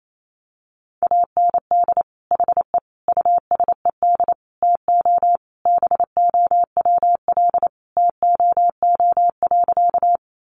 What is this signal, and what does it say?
Morse code, heard as a keyed tone, about 28 wpm, 700 Hz. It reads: AND HE USED TO BOWL TOO.
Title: Sentences from Top 1200 Words 28wpm